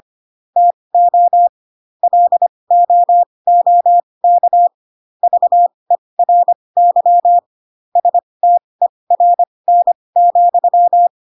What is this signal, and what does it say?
Morse code, heard as a keyed tone, about 25 wpm, spaced as for 20 wpm, 700 Hz. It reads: TO LOOK VERY STERN,
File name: Pinocchio / Pinocchio ch8 15 25 ik1hge